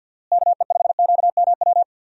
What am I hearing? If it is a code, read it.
Morse code, heard as a keyed tone, about 51 wpm, 700 Hz. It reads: KE5QMW
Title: USA Call Signs - 6 Characters - Repeated 3x - 50wpm